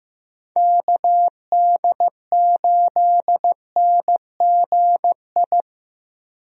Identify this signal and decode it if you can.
Morse code, heard as a keyed tone, about 15 wpm, 700 Hz. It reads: KD8NGI
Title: USA Call Signs - 6 Characters - Repeated 2x - 15wpm